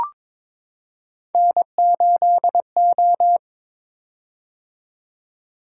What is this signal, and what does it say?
Morse code, heard as a keyed tone, about 23 wpm, 700 Hz. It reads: N8O
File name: practice / 22wpm / Three Letter-Number Combinations - 22wpm